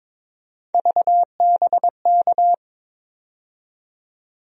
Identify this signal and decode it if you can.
Morse code, heard as a keyed tone, about 21 wpm, 700 Hz. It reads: VBK
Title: USA Call Sign Postfixes - 3 Letters 22wpm